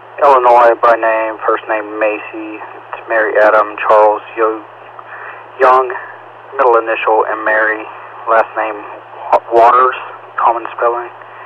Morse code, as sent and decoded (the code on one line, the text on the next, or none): none